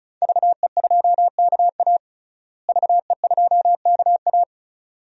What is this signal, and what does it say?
Morse code, heard as a keyed tone, about 34 wpm, 700 Hz. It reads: VE2KA VE2KA
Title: DXCC-100 Call Signs - 5 Characters - Repeated 2x - 35wpm